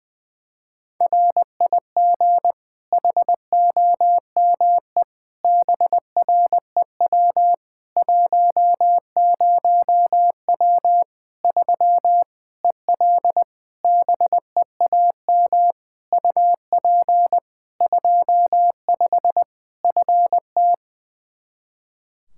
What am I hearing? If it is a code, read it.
Morse code, heard as a keyed tone, about 21 wpm, 700 Hz. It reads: RIG HOME BREW 10W 3 EL BEAM UP 25 FT
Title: K1USN SST QSO Practice 20wpm